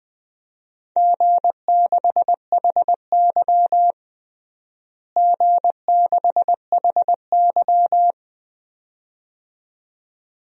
Morse code, heard as a keyed tone, about 20 wpm, 700 Hz. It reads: G6HY G6HY
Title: DXCC-100 Call Signs - 4 Characters - Repeated 2x - 20wpm